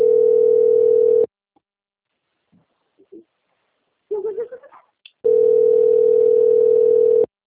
Ringback tone (North America), 440 + 480 Hz.